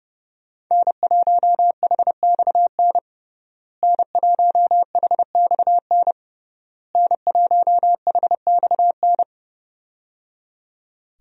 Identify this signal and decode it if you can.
Morse code, heard as a keyed tone, about 30 wpm, 700 Hz. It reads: N1HXN N1HXN N1HXN
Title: USA Call Signs - 5 Characters - Repeated 3x - 30wpm